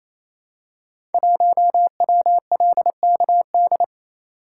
Morse code, heard as a keyed tone, about 29 wpm, 700 Hz. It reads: 1WLKD
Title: Code Groups - 5 Letter:Numbers - 28wpm